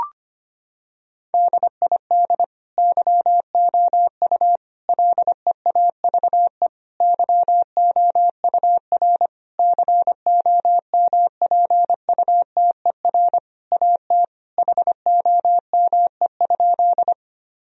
Morse code, heard as a keyed tone, about 25 wpm, 700 Hz. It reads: DID YOU LEAVE YOUR COMPUTER AT HOME?
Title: Sentences from Top 1000 Words - Review Part 3 - 25wpm